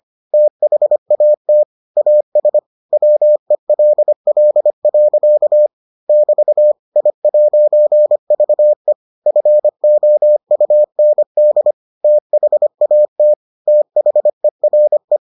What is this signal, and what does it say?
Morse code, heard as a keyed tone, about 25 wpm, 600 Hz. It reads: THAT AS WELL. <BT> I'VE FOUND THAT THERE